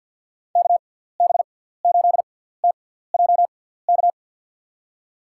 Morse code, heard as a keyed tone, about 50 wpm, spaced as for 17 wpm, 700 Hz. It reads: KB8TJK
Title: USA Call Signs - 6 Characters - 6x Character Spacing - 50wpm